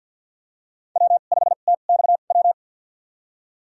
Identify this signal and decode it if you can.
Morse code, heard as a keyed tone, about 52 wpm, spaced as for 36 wpm, 700 Hz. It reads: W5TXW
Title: USA Call Signs - 5 Characters - 2x Character Spacing - 50wpm